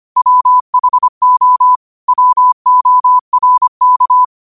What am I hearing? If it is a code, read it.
Morse code, heard as a keyed tone, about 25 wpm, 1000 Hz. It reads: WHO WORK
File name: qotdfiles / 25wpm